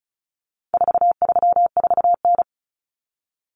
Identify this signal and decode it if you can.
Morse code, heard as a keyed tone, about 34 wpm, 700 Hz. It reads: 434N
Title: Common Cut Numbers - 4-Digit 35wpm